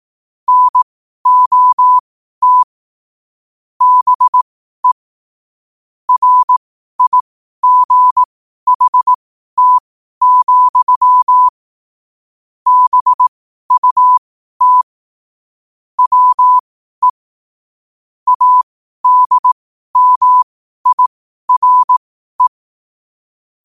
Morse code, heard as a keyed tone, about 18 wpm, spaced as for 12 wpm, 1000 Hz. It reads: NOT BE RIGHT, BUT WE ADMIRE
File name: qotdfiles / 12wpm